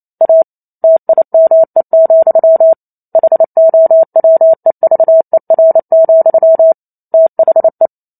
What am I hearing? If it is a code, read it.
Morse code, heard as a keyed tone, about 28 wpm, 650 Hz. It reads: A TIME, HOWEVER, THE